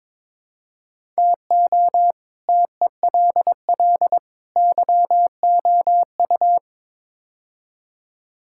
Morse code, heard as a keyed tone, about 22 wpm, 700 Hz. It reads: TO TELL YOU
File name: practice / 22wpm / Build A Phrase - 5-Word Phrases - 22wpm